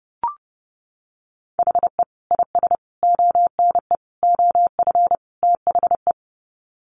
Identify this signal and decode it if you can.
Morse code, heard as a keyed tone, about 28 wpm, 700 Hz. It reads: HE IS ONE OF THE
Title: Sets of 5 Words Encore 30wpm